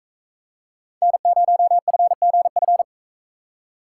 Morse code, heard as a keyed tone, about 41 wpm, 700 Hz. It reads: N0FGF